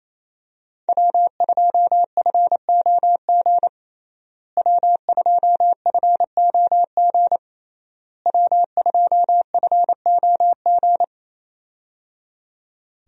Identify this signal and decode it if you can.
Morse code, heard as a keyed tone, about 27 wpm, 700 Hz. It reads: W2FOG W2FOG W2FOG
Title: USA Call Signs - 5 Characters - Repeated 3x - 28wpm